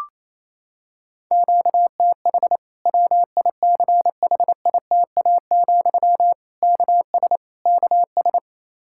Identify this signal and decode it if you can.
Morse code, heard as a keyed tone, about 28 wpm, 700 Hz. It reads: QTH WICHITA, KS KS
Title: Practice QSO - Exchange 1 - Beginning - 28wpm